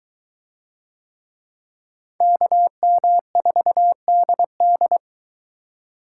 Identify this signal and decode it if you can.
Morse code, heard as a keyed tone, about 22 wpm, 700 Hz. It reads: KM4DD